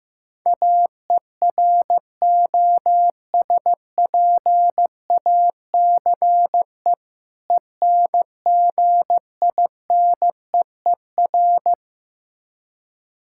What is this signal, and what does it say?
Morse code, heard as a keyed tone, about 15 wpm, 700 Hz. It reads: AEROSPACE ENGINEER
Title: Practice QSO - Exchange 3 - More Info - 15wpm